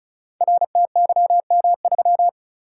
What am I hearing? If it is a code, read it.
Morse code, heard as a keyed tone, about 36 wpm, 700 Hz. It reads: RTYM3
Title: Code Groups - 5 Letter:Numbers - 35wpm